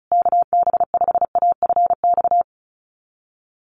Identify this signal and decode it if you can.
Morse code, heard as a keyed tone, about 34 wpm, 700 Hz. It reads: KB5AFX
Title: USA Call Signs Repeated 3x 35wpm